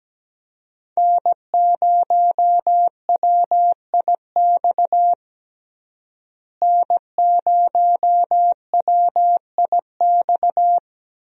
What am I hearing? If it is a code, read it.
Morse code, heard as a keyed tone, about 17 wpm, 700 Hz. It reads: N0WIX N0WIX
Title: USA Call Signs - 5 Characters - Repeated 3x - 17wpm